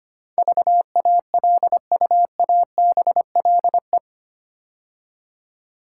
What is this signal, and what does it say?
Morse code, heard as a keyed tone, about 25 wpm, 690 Hz. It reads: VALUABLE
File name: 8 Letter Words - First 3 Letters Given - 25wpm